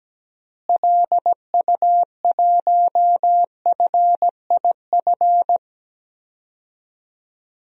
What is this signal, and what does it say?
Morse code, heard as a keyed tone, about 17 wpm, 700 Hz. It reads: LU1FIF
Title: DXCC-100 Call Signs - 6 Characters - Repeated 2x - 17wpm